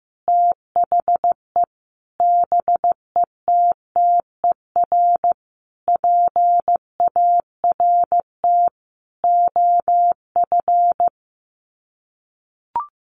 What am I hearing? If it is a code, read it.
Morse code, heard as a keyed tone, about 15 wpm, 700 Hz. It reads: THE BETTER PART OF
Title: Sets of 5 Words Encore2 15wpm